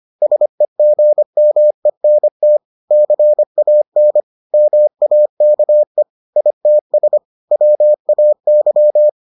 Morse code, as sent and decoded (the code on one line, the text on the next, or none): ... --. -- . -. - / -.-. .- -. / -- .- -.- . / .. - ... / .-- .- -.--
SEGMENT CAN MAKE ITS WAY